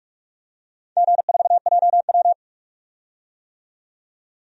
Morse code, heard as a keyed tone, about 46 wpm, 700 Hz. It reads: G4JW
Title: DXCC-100 Call Signs - 4 Characters - 45wpm